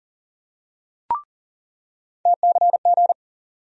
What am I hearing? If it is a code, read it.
Morse code, heard as a keyed tone, about 40 wpm, 700 Hz. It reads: TCG
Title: USA Call Sign Postfixes - 3 Letters 40wpm